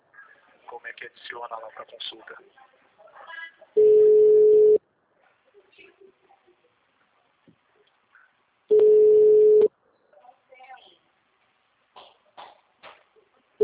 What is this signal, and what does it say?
Ringback tone (Europe, 425 Hz), with speech.
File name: out- -2002- - - .742